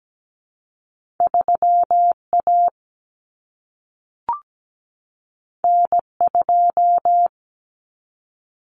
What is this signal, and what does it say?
Morse code, heard as a keyed tone, about 17 wpm, 700 Hz. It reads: 3A N2
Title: Common Cut Numbers - 2-Digit 17wpm